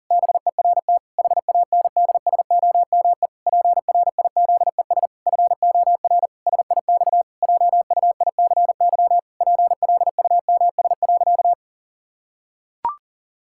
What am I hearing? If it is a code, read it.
Morse code, heard as a keyed tone, about 40 wpm, 700 Hz. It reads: BERT HANDSOME PRIZES FOR SIX JUICY PLUMS.
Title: Pangram Challenge v2 - 40wpm